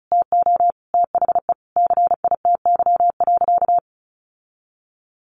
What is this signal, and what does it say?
Morse code, heard as a keyed tone, about 34 wpm, 700 Hz. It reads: TO THE CITY.
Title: Sentences from Top 1400 Words 35wpm